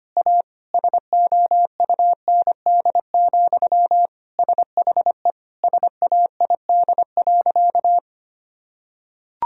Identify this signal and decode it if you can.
Morse code, heard as a keyed tone, about 25 wpm, 700 Hz. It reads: A SOUND, SHE SAID.
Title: Sentences from Top 1000 Words - Review Part 1 - 25wpm